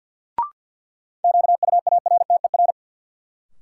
Morse code, heard as a keyed tone, about 52 wpm, 700 Hz. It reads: QUARTER
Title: Sentences from Top 1300 Words - New Word Drill - 50wpm